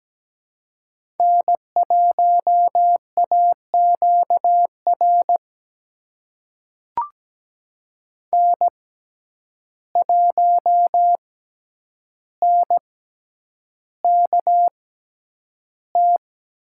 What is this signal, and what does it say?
Morse code, heard as a keyed tone, about 17 wpm, spaced as for 9 wpm, 700 Hz. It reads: N1AQR N1NKT
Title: USA Call Signs - 5 Characters - 6x Character Spacing - 17wpm